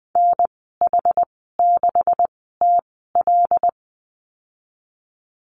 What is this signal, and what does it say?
Morse code, heard as a keyed tone, about 20 wpm, spaced as for 15 wpm, 700 Hz. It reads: NH6TL
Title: 128 Lesson 21g - Number 6 - Call Signs 20wpm